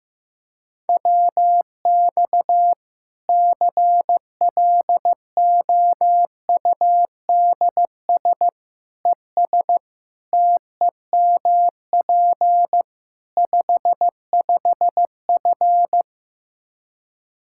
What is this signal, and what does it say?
Morse code, heard as a keyed tone, about 16 wpm, 700 Hz. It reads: WX CLOUDS ES TEMP 55F
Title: Practice QSO - Exchange 2 - Info - 15wpm